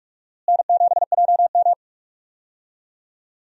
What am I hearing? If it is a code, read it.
Morse code, heard as a keyed tone, about 46 wpm, 700 Hz. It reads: N7JM
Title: USA Call Signs - 4 Characters - Repeated 3x - 45wpm